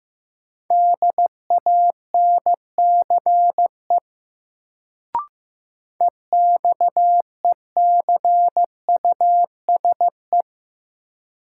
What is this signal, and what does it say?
Morse code, heard as a keyed tone, about 15 wpm, 700 Hz. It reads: DANCE EXECUSE
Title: Sentences from Top 700 Words - New Word Drill - 15wpm